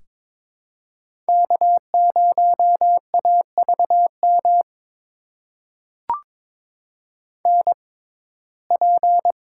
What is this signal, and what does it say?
Morse code, heard as a keyed tone, about 22 wpm, spaced as for 16 wpm, 700 Hz. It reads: K0AVM NP
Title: USA Call Signs - 5 Characters - 6x Character Spacing - 22wpm